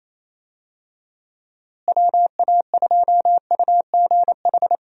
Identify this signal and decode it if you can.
Morse code, heard as a keyed tone, about 28 wpm, 700 Hz. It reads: WA2UGH
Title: USA Call Signs - 6 Characters - Repeated 2x - 28wpm